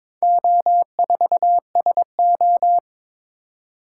Morse code, heard as a keyed tone, about 22 wpm, 700 Hz. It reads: O4SO
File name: practice / 22wpm / Four Letter-Number Combinations - 22wpm